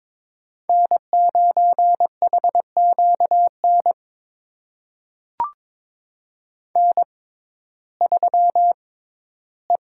Morse code, heard as a keyed tone, about 21 wpm, spaced as for 13 wpm, 700 Hz. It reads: N9HQN N3E
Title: USA Call Signs - 6x Character Spacing - 22wpm